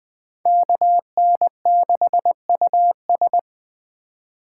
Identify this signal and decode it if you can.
Morse code, heard as a keyed tone, about 21 wpm, 700 Hz. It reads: KN6US